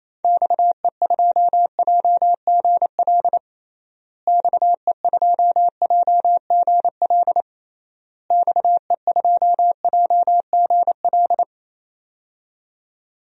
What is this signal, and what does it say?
Morse code, heard as a keyed tone, about 28 wpm, 700 Hz. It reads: XE2JGL XE2JGL XE2JGL
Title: DXCC-100 Call Signs - 6 Characters - Repeated 3x - 28wpm